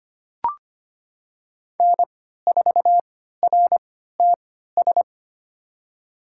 Morse code, heard as a keyed tone, about 25 wpm, spaced as for 14 wpm, 700 Hz. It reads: N4RTS
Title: USA Call Signs - 5 Characters - 3x Character Spacing - 25wpm